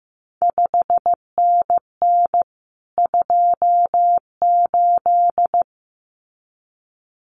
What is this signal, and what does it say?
Morse code, heard as a keyed tone, about 15 wpm, 700 Hz. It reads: 5NN 28
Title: CQ WW DX Contest Exchange Encore - 15wpm